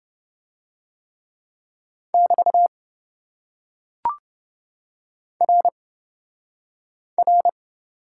Morse code, heard as a keyed tone, about 27 wpm, 700 Hz. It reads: <BT> R R